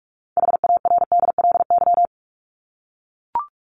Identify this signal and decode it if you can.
Morse code, heard as a keyed tone, about 44 wpm, 700 Hz. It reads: HARDLY
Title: LY Suffix - Top 100 Words with Base Words 45wpm